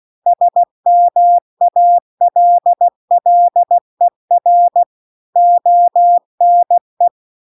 Morse code, heard as a keyed tone, about 16 wpm, 700 Hz. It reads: SMALLER ONE
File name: CW-The-Big-Twoheaded-River-2-16WPM